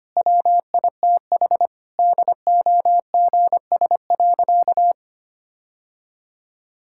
Morse code, heard as a keyed tone, about 25 wpm, 700 Hz. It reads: WITH DOGS.
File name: Easy Sentences with Hints - 25wpm